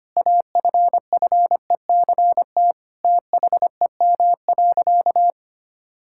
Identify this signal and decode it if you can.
Morse code, heard as a keyed tone, about 25 wpm, 700 Hz. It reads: AFFECT THEM.
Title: Sentences from Top 1000 Words - Review Part 6 - 25wpm